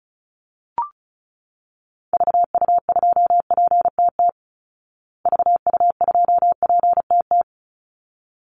Morse code, heard as a keyed tone, about 34 wpm, 700 Hz. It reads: VU2PTT VU2PTT
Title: Call Signs Repeated 3x - 35wpm